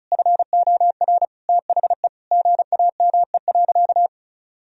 Morse code, heard as a keyed tone, about 34 wpm, 700 Hz. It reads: FOR THE GAME.
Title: Sentences from Top 1000 Words - Review Part 4 - 35wpm